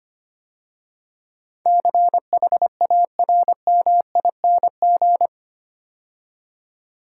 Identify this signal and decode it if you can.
Morse code, heard as a keyed tone, about 25 wpm, 700 Hz. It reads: CHARMING
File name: 8 Letter Words - First 3 Letters Given - 25wpm